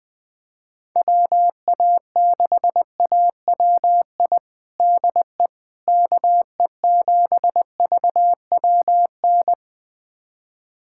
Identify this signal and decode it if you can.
Morse code, heard as a keyed tone, about 21 wpm, 700 Hz. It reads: WA6AWI DE KE7VWN
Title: USA Callsign DE Callsign - 6 Characters - 20wpm